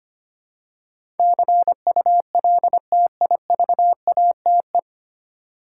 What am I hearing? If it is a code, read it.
Morse code, heard as a keyed tone, about 25 wpm, 700 Hz. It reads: CULTIVATE
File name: practice / 25wpm / 9 Letter Words - First 3 Letters Given - 25wpm